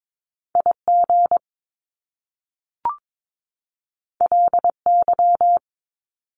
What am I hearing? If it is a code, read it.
Morse code, heard as a keyed tone, about 22 wpm, 700 Hz. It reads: IG LY
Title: Top Two Letter Combinations in Words 22wpm